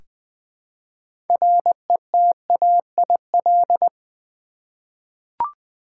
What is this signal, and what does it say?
Morse code, heard as a keyed tone, about 20 wpm, 700 Hz. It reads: RETAIL
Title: 6 Letter Words - First 3 Letters Given - 20wpm